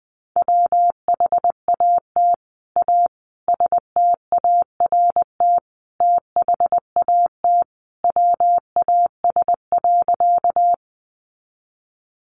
Morse code, heard as a keyed tone, about 20 wpm, 700 Hz. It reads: WHAT A START THAT WAS.
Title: Sentences from Top 300 Words 20wpm